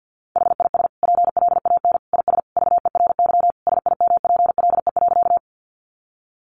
Morse code, heard as a keyed tone, about 50 wpm, 700 Hz. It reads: HIS PLAN IS VERY SIMPLE.
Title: Sentences from Top 500 Words 50wpm